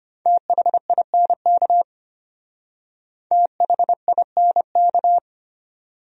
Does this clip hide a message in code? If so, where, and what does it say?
Yes: Morse code from 0.3 to 5.2 s, heard as a keyed tone, about 32 wpm, 700 Hz. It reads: THINK THINK